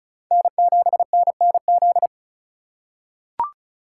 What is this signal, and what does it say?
Morse code, heard as a keyed tone, about 34 wpm, 700 Hz. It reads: N7NNZ
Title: USA Call Signs - 5 Characters - 3x Character Spacing - 35wpm